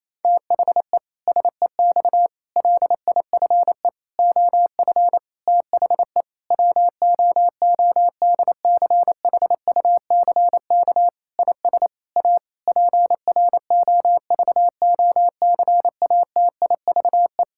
Morse code, heard as a keyed tone, about 28 wpm, 700 Hz. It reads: THE SEX LIFE OF THE WOODCHUCK IS A PROVOCATIVE
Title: Pangram Challenge v2 - 28wpm